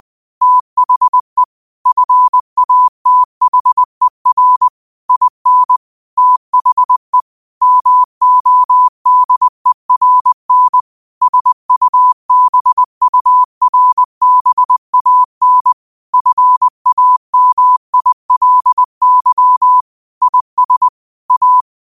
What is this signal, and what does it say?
Morse code, heard as a keyed tone, about 20 wpm, 1000 Hz. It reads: THE FATHER IN THE MODERN SUBURBAN FAMILY IS A
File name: qotdfiles / 20wpm